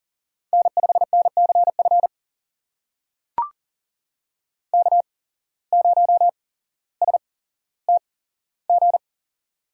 Morse code, heard as a keyed tone, about 43 wpm, spaced as for 17 wpm, 700 Hz. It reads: N5NCF K0STG
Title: USA Call Signs - 8x Character Spacing - 40wpm